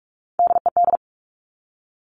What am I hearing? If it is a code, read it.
Morse code, heard as a keyed tone, about 46 wpm, 700 Hz. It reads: DED
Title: Top Three Letter Combinations in Words 45wpm